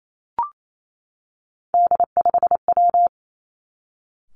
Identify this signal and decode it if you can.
Morse code, heard as a keyed tone, about 28 wpm, 700 Hz. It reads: D5W
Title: Three Challenging Letter-Number Combinations 28wpm